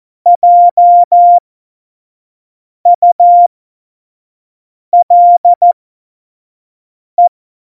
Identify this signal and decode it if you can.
Morse code, heard as a keyed tone, about 14 wpm, spaced as for 5 wpm, 700 Hz. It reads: JULE
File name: Morsekurs / Witze / Witz 11. Juni 2019, 25 F70